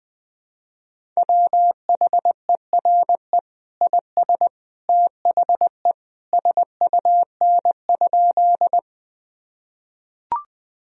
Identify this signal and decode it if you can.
Morse code, heard as a keyed tone, about 20 wpm, 700 Hz. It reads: WHERE IS THE SUN?
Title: Sentences from Top 1000 Words - Review Part 1 - 20wpm